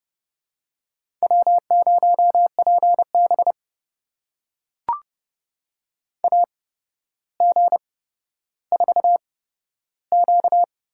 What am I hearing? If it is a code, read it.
Morse code, heard as a keyed tone, about 30 wpm, spaced as for 13 wpm, 700 Hz. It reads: W0PB AG4Q
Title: USA Call Signs - 8x Character Spacing - 30wpm